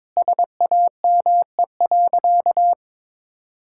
Morse code, heard as a keyed tone, about 22 wpm, 700 Hz. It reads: SAME.
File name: Sentences with 9-Word Phrases - 22wpm